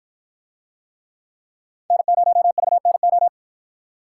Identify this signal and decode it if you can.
Morse code, heard as a keyed tone, about 53 wpm, 700 Hz. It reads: N0VNO